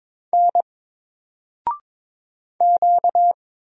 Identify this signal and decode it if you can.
Morse code, heard as a keyed tone, about 22 wpm, 700 Hz. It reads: N Q
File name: Single-Letter-Number-22wpm